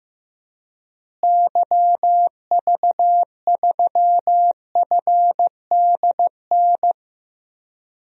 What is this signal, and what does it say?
Morse code, heard as a keyed tone, about 15 wpm, 700 Hz. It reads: YV3FDN
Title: DXCC-100 Call Signs - 6 Characters - Repeated 3x - 15wpm